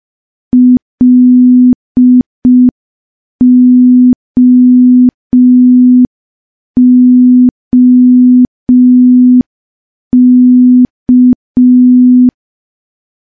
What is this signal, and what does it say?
Morse code, heard as a keyed tone, about 5 wpm, 250 Hz. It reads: LOOK
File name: morse / morse-training-05WPM